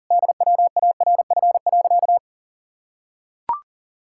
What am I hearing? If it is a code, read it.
Morse code, heard as a keyed tone, about 40 wpm, 700 Hz. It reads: DWARF.
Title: Pangram Challenge v2 - 40wpm